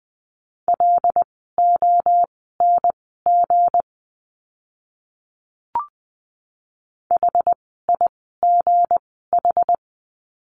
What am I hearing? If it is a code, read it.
Morse code, heard as a keyed tone, about 20 wpm, spaced as for 14 wpm, 700 Hz. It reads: LONG HIGH
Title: 160 Lesson 26f - Letter g - Words 20wpm